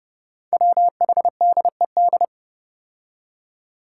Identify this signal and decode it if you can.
Morse code, heard as a keyed tone, about 30 wpm, 700 Hz. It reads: WHDED
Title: Five Challenging Letter-Number Combinations 30wpm